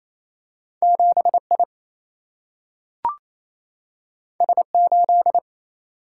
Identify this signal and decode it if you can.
Morse code, heard as a keyed tone, about 28 wpm, 700 Hz. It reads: 7 I S 8
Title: Single-Letter Single-Number 28wpm